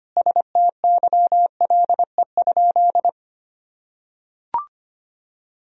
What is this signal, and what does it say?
Morse code, heard as a keyed tone, about 25 wpm, 700 Hz. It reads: STYLE?
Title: Sentences from Top 1000 Words - Review Part 4 - 25wpm